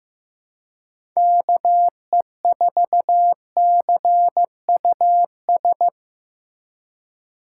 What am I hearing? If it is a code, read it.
Morse code, heard as a keyed tone, about 15 wpm, 700 Hz. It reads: KE4CUS
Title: USA Call Signs - 6 Characters - 4x Character Spacing - 15wpm